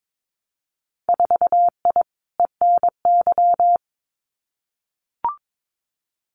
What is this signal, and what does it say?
Morse code, heard as a keyed tone, about 22 wpm, 700 Hz. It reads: 4I ENY
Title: Winter Field Day Exchanges Encore 22wpm